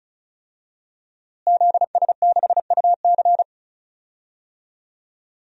Morse code, heard as a keyed tone, about 36 wpm, 700 Hz. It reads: ZS6UC
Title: DXCC-100 Call Signs - 5 Characters - Repeated 2x - 35wpm